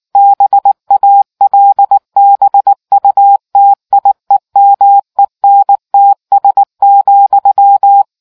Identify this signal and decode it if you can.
Morse code, heard as a keyed tone, about 19 wpm, 800 Hz. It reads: BALBUTIEMENTS,